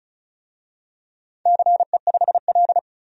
Morse code, heard as a keyed tone, about 34 wpm, 690 Hz. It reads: CE5L
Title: DXCC-100 Call Signs - 4 Characters - 35wpm